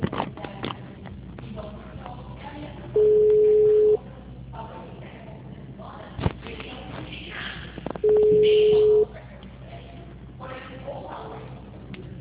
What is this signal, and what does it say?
Ringback tone (Europe, 425 Hz).